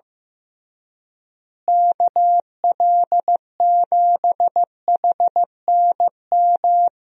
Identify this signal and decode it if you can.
Morse code, heard as a keyed tone, about 15 wpm, 700 Hz. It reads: KL7HNM
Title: USA Call Signs - 6 Characters - Repeated 3x - 15wpm